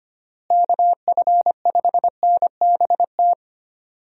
Morse code, heard as a keyed tone, about 25 wpm, 700 Hz. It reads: KF5NBT
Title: USA Call Signs - 6 Characters - Repeated 3x - 25wpm